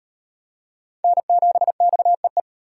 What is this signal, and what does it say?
Morse code, heard as a keyed tone, about 34 wpm, 700 Hz. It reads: N7XEE